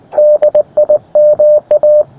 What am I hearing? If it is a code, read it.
Morse code, heard as a keyed tone, about 19 wpm, 600 Hz. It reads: DIMA
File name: attachments / rd7k 20m